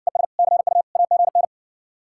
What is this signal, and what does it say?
Morse code, heard as a keyed tone, about 60 wpm, 700 Hz. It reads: ES CU AGN